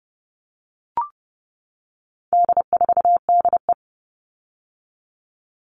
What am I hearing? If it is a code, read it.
Morse code, heard as a keyed tone, about 30 wpm, 700 Hz. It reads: D4DE
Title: All Cut Numbers - 4-Digit 30wpm